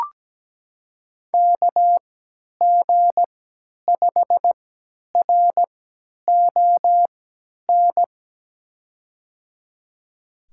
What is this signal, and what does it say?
Morse code, heard as a keyed tone, about 17 wpm, spaced as for 10 wpm, 700 Hz. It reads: KG5RON
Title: USA Call Signs - 6 Characters - 3x Character Spacing - 17wpm